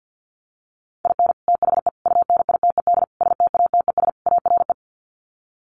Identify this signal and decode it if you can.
Morse code, heard as a keyed tone, about 49 wpm, 700 Hz. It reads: IN THE UNITED STATES ARE
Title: Sets of 5 Words Encore 50wpm